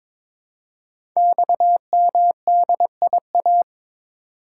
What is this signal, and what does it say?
Morse code, heard as a keyed tone, about 22 wpm, 700 Hz. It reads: XMDIA
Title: Code Groups - 5 Letters - 22wpm